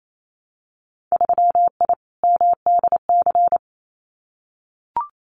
Morse code, heard as a keyed tone, about 28 wpm, 700 Hz. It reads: 3I MDC
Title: Winter Field Day Exchanges Encore 28wpm